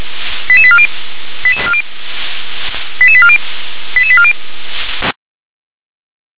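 Five-tone selective call (ZVEI), heard as quick runs of five tones, tones of about 70 ms. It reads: 88840 88840 88840
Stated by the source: ZVEI five-tone selective call received on SDR.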